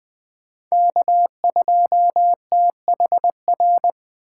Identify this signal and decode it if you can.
Morse code, heard as a keyed tone, about 21 wpm, 700 Hz. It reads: K2THR
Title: USA Call Signs - 5 Characters - 3x Character Spacing - 20wpm